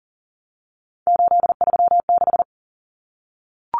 Morse code, heard as a keyed tone, about 40 wpm, 700 Hz. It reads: 836
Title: Numbers - 3-Digit 40wpm